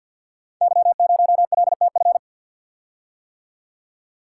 Morse code, heard as a keyed tone, about 47 wpm, 700 Hz. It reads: Y0LTF
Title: Code Groups - 5 Letter:Numbers - 50wpm